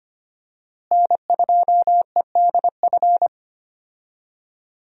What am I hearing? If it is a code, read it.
Morse code, heard as a keyed tone, about 25 wpm, 700 Hz. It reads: N2EDF
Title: USA Call Signs - 5 Characters - Repeated 3x - 25wpm